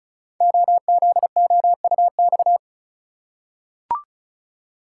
Morse code, heard as a keyed tone, about 36 wpm, 690 Hz. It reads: OZOUX
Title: Code Groups - 5 Letter:Numbers - 35wpm